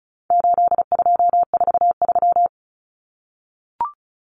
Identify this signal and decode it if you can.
Morse code, heard as a keyed tone, about 34 wpm, 700 Hz. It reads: 8243
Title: Common Cut Numbers - 4-Digit 35wpm